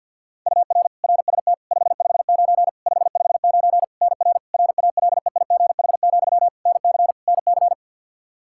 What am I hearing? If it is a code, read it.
Morse code, heard as a keyed tone, about 53 wpm, 700 Hz. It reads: UR RST 559 559 NR RALEIGH, NC NC
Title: K1USN SST QSO Practice 50wpm